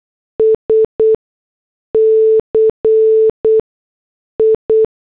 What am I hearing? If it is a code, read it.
Morse code, heard as a keyed tone, about 8 wpm, spaced as for 6 wpm, 440 Hz. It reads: SCI